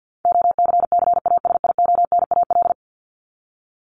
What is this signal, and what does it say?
Morse code, heard as a keyed tone, about 48 wpm, 700 Hz. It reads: OCCASIONAL
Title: LY Suffix - Top 100 Words with Base Words 50wpm